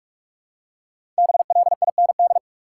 Morse code, heard as a keyed tone, about 47 wpm, 700 Hz. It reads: BLIND
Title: Five Letter Words - 45wpm